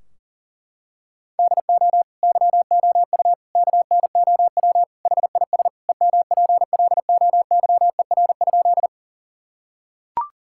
Morse code, heard as a keyed tone, about 40 wpm, 700 Hz. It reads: DO YOU KNOW HIS EMPLOYER?
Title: Sentences from Top 1600 Words 40wpm